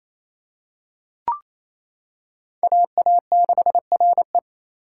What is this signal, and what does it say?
Morse code, heard as a keyed tone, about 27 wpm, 700 Hz. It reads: AA6RE
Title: K1USN SST QSO Practice - 28wpm